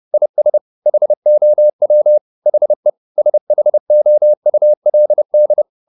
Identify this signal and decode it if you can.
Morse code, heard as a keyed tone, about 30 wpm, 600 Hz. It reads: IS HOW HE SHOULD
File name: the old man and the sea / the old man and the sea 1 30wpm